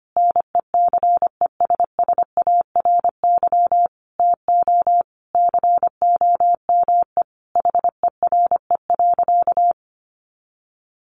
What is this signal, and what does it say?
Morse code, heard as a keyed tone, about 25 wpm, 700 Hz. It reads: NECESSARY TO COME HERE.
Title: Sentences from Top 1300 Words 25wpm